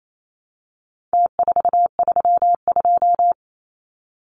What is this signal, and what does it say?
Morse code, heard as a keyed tone, about 28 wpm, 700 Hz. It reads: T432
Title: Numbers - Contest Style - 4-Digit 28wpm